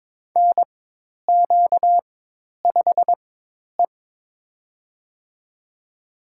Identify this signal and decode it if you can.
Morse code, heard as a keyed tone, about 23 wpm, spaced as for 10 wpm, 700 Hz. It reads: NQ5E
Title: USA Call Signs - 4 Characters - 4x Character Spacing - 22wpm